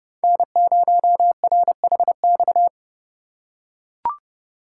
Morse code, heard as a keyed tone, about 29 wpm, 700 Hz. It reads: N0RHX
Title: USA Call Signs - 5 Characters - 2x Character Spacing - 30wpm